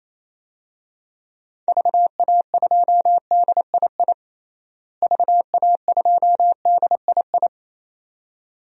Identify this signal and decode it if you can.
Morse code, heard as a keyed tone, about 28 wpm, 700 Hz. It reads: VA2DII VA2DII
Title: DXCC-100 Call Signs - 6 Characters - Repeated 2x - 28wpm